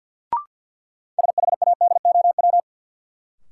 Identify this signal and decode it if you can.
Morse code, heard as a keyed tone, about 52 wpm, 700 Hz. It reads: SHADOW
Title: Sets of 5 Words Encore - New Word Drill - 50wpm